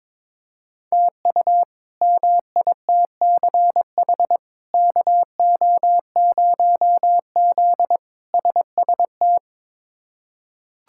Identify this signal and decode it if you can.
Morse code, heard as a keyed tone, about 22 wpm, 700 Hz. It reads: TU MITCH KO0Z SST
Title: K1USN SST QSO Practice - 22wpm